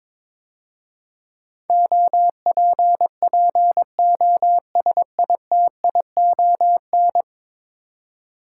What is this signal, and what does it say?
Morse code, heard as a keyed tone, about 22 wpm, 700 Hz. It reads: OPPOSITION
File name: practice / 22wpm / 10 Letter Words - First 3 Letters Given - 22wpm